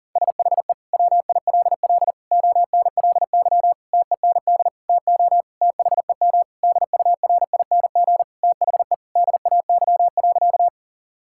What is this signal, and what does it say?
Morse code, heard as a keyed tone, about 40 wpm, 700 Hz. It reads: SHE WILL ONLY TEND TO THEM DURING THE DAY.
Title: Sentences from Top 1000 Words - Review Part 5 - 40wpm